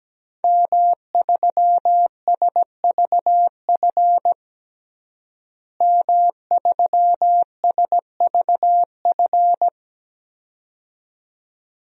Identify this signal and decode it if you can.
Morse code, heard as a keyed tone, about 17 wpm, 700 Hz. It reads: M3SVF M3SVF
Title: DXCC-100 Call Signs - 5 Characters - Repeated 2x - 17wpm